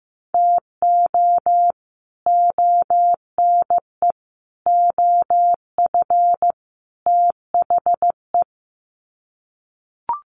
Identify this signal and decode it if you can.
Morse code, heard as a keyed tone, about 15 wpm, 700 Hz. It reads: TO ONE OF THE
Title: Sets of 4 Words 15wpm